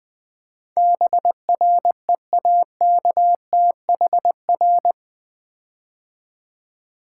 Morse code, heard as a keyed tone, about 21 wpm, 700 Hz. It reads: BREAKTHR
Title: Build A Word - 4 Letters At A Time - 12-Letter Words - 20wpm